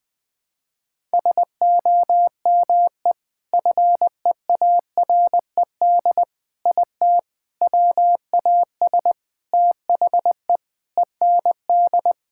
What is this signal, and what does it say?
Morse code, heard as a keyed tone, about 20 wpm, 700 Hz. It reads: SOME FEARED IT WAS THE END
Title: Sentences with 5-Word Phrases - Spoken Beginning - 20wpm